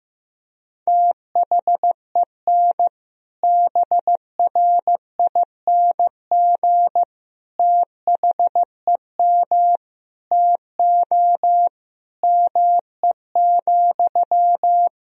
Morse code, heard as a keyed tone, about 15 wpm, 700 Hz. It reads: THEN BRING THEM TO ME,
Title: Sentences from Top 500 Words and Abbreviations 15wpm